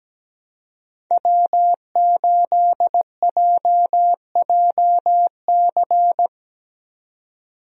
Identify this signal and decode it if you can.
Morse code, heard as a keyed tone, about 17 wpm, 700 Hz. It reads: W8JJC
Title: USA Call Signs - 5 Characters - 17wpm